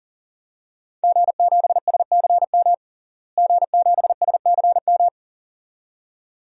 Morse code, heard as a keyed tone, about 40 wpm, 700 Hz. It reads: G7SCM G7SCM
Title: DXCC-100 Call Signs - 5 Characters - Repeated 2x - 40wpm